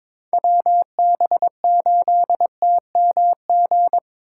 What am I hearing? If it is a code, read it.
Morse code, heard as a keyed tone, about 26 wpm, 700 Hz. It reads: WB8TMG
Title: Speed Racing - USA 2x3 Call Signs 22wpm